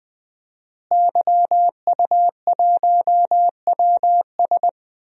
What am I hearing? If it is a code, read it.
Morse code, heard as a keyed tone, about 20 wpm, 700 Hz. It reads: YU1WS
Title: DXCC-100 Call Signs - 5 Characters - Repeated 2x - 20wpm